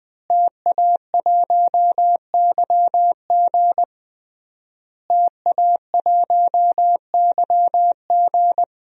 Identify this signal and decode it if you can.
Morse code, heard as a keyed tone, about 20 wpm, 700 Hz. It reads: TA1YG TA1YG
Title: DXCC-100 Call Signs - 5 Characters - Repeated 3x - 20wpm